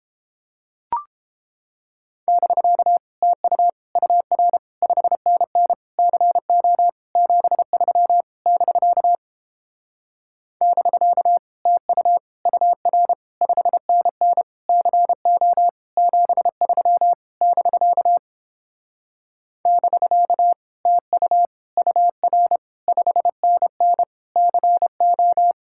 Morse code, heard as a keyed tone, about 29 wpm, 700 Hz. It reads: <BK> TU UR 5NN CO 73 <BK> <BK> TU UR 5NN CO 73 <BK> <BK> TU UR 5NN CO